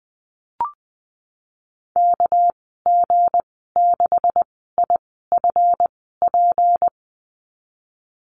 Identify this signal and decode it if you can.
Morse code, heard as a keyed tone, about 20 wpm, spaced as for 14 wpm, 700 Hz. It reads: KG6IFP
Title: 245 Lesson 39g - Letter z - Call Signs 20wpm